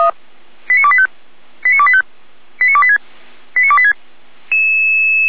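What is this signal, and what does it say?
Five-tone selective call (ZVEI), heard as quick runs of five tones, tones of about 70 ms. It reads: 89285 89285 89286 89286
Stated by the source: ZVEI five-tone call received by SDR, hosted by Feuerwehr Orlamünde (f85360).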